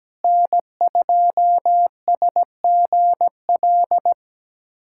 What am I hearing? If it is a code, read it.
Morse code, heard as a keyed tone, about 17 wpm, 700 Hz. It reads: N2SGL
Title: USA Callsign DE Callsign - 5 Characters - 17wpm